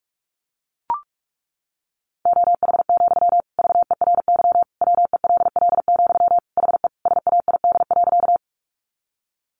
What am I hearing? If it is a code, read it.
Morse code, heard as a keyed tone, about 44 wpm, 700 Hz. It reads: OH, VERY WELL, HE SAID.
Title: Sentences from Top 500 Words 45wpm